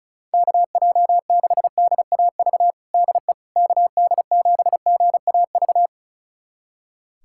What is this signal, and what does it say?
Morse code, heard as a keyed tone, about 34 wpm, 700 Hz. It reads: KJ6DAV DE KD7GAV
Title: USA Callsign DE Callsign - 6 Characters - 35wpm